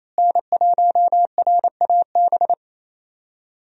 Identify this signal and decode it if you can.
Morse code, heard as a keyed tone, about 28 wpm, 700 Hz. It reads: N1RAB
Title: USA Call Signs - 5 Characters - Repeated 2x - 28wpm